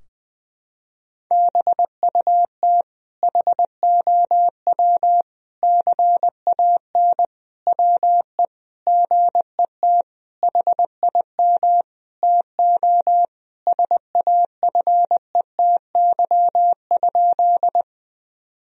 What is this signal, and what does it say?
Morse code, heard as a keyed tone, about 20 wpm, 700 Hz. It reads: BUT HOW CAN WE GET HIM TO SAFETY?
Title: Sentences from Top 1000 Words - Review Part 4 - 20wpm